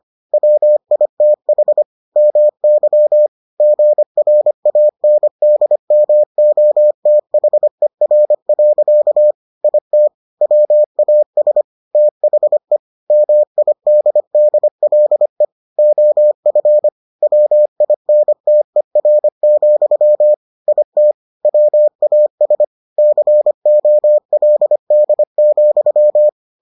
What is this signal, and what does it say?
Morse code, heard as a keyed tone, about 25 wpm, 600 Hz. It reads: WITH MY GRANDMOTHER. IT WAS THE MIDDLE OF WINTER, IT WAS COLD,